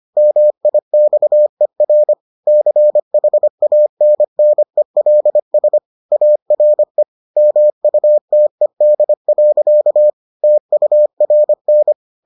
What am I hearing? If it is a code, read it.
Morse code, heard as a keyed tone, about 25 wpm, 600 Hz. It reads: MIXER CHANNELS ARE MUTED. TURN